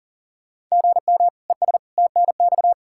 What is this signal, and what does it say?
Morse code, heard as a keyed tone, about 40 wpm, 700 Hz. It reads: GM ES TNX
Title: Practice QSO - Exchange 1 - Beginning - 40wpm